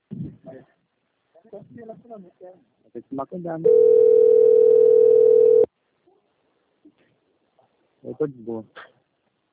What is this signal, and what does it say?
Ringback tone (North America), 440 + 480 Hz, with speech.
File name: rg-1600-unknown- - - .19500